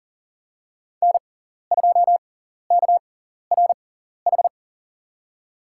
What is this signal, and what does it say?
Morse code, heard as a keyed tone, about 40 wpm, spaced as for 14 wpm, 700 Hz. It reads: N2KRH
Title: USA Call Signs - 6x Character Spacing - 40wpm